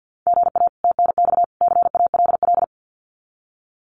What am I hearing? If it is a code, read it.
Morse code, heard as a keyed tone, about 50 wpm, 700 Hz. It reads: GA TNX CALL
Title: Common QSO Phrases 50wpm